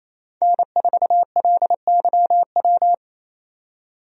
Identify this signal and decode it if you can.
Morse code, heard as a keyed tone, about 28 wpm, 700 Hz. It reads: N4LYW
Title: USA Call Signs - 5 Characters - 28wpm